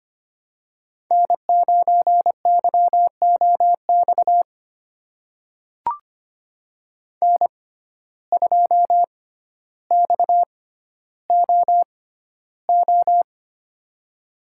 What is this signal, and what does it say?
Morse code, heard as a keyed tone, about 25 wpm, spaced as for 13 wpm, 700 Hz. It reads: N9YOX N2XOO
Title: USA Call Signs - 5 Characters - 6x Character Spacing - 25wpm